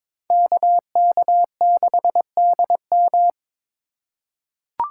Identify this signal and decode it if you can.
Morse code, heard as a keyed tone, about 22 wpm, 700 Hz. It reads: KK6DM
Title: USA Call Signs - 5 Characters - 2x Character Spacing - 22wpm